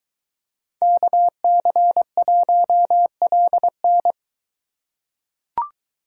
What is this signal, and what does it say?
Morse code, heard as a keyed tone, about 23 wpm, 700 Hz. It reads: KC1LN